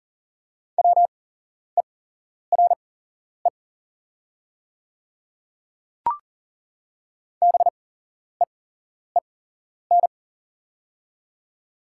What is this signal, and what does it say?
Morse code, heard as a keyed tone, about 40 wpm, spaced as for 11 wpm, 700 Hz. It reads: WERE BEEN
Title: 8x Character Spacing - Top 100 Words - 40wpm